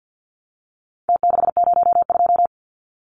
Morse code, heard as a keyed tone, about 49 wpm, 700 Hz. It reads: T602
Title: Numbers - Contest Style - 4-Digit 50wpm